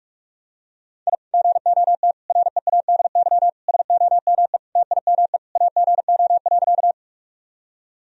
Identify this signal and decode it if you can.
Morse code, heard as a keyed tone, about 42 wpm, 700 Hz. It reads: I GOT READY SOME TIME AGO.
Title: Sentences from Top 1000 Words - Review Part 3 - 45wpm